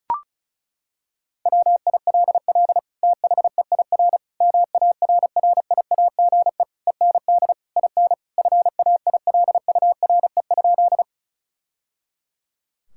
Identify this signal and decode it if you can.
Morse code, heard as a keyed tone, about 34 wpm, 700 Hz. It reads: WILL THEIR MARRIAGE END IN FAILURE?
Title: Sentences from Top 1000 Words - Review Part 6 - 35wpm